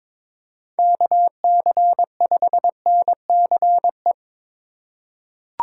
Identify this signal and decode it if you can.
Morse code, heard as a keyed tone, about 22 wpm, 700 Hz. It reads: KC5NCE
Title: USA Call Signs - 6 Characters - 2x Character Spacing - 22wpm